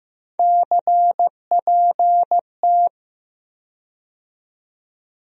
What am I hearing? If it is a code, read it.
Morse code, heard as a keyed tone, about 15 wpm, 700 Hz. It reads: CPT
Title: USA Call Sign Postfixes - 3 Letters 15wpm